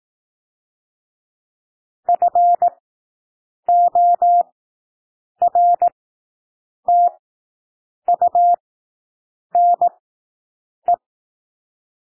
Morse code, heard as a keyed tone, about 18 wpm, spaced as for 7 wpm, 700 Hz. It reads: FORTUNE
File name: SOUND WORDS 18 wpm